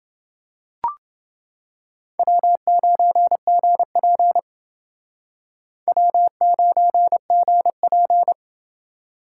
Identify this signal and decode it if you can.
Morse code, heard as a keyed tone, about 29 wpm, 700 Hz. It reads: W9GP W9GP